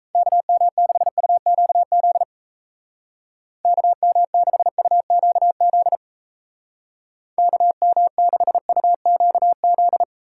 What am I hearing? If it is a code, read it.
Morse code, heard as a keyed tone, about 34 wpm, 700 Hz. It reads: KM6UQZ KM6UQZ KM6UQZ
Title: Speed Racing - USA 2x3 Call Signs 28wpm